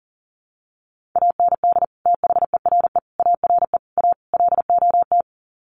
Morse code, heard as a keyed tone, about 40 wpm, 700 Hz. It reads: AND THERE ARE A LOT
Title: Sets of 5 Words Encore2 40wpm